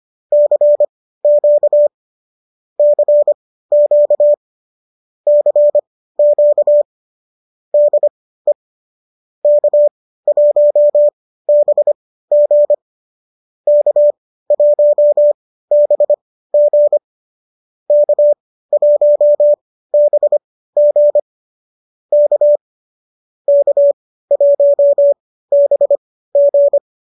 Morse code, heard as a keyed tone, about 25 wpm, spaced as for 15 wpm, 590 Hz. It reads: CQ CQ CQ DE K1BG K1BG K1BG K K1BG
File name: qso201 15